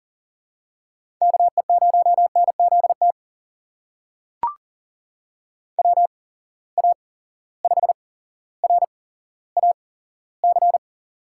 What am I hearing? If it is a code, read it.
Morse code, heard as a keyed tone, about 39 wpm, spaced as for 17 wpm, 700 Hz. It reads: KE0NZT WA5RAC
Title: USA Call Signs - 6 Characters - 8x Character Spacing - 40wpm